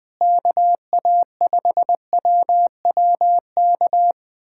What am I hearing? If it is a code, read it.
Morse code, heard as a keyed tone, about 20 wpm, 700 Hz. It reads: KA5WWK
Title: USA Callsign DE Callsign 20wpm